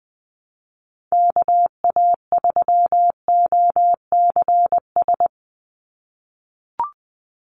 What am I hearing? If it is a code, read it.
Morse code, heard as a keyed tone, about 20 wpm, 700 Hz. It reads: KA3OCS
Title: Call Signs - 20wpm